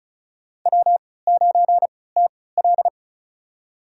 Morse code, heard as a keyed tone, about 34 wpm, spaced as for 20 wpm, 700 Hz. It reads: W9TL
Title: USA Call Signs - 3x Character Spacing - 35wpm